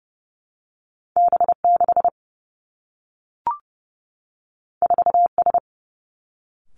Morse code, heard as a keyed tone, about 30 wpm, 700 Hz. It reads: B 6 4 S
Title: Two Challenging Letter-Number Combinations 30wpm